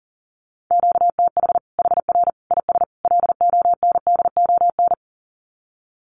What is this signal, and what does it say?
Morse code, heard as a keyed tone, about 40 wpm, 700 Hz. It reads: QTH HR IS LONDON
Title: Common QSO Phrases 40wpm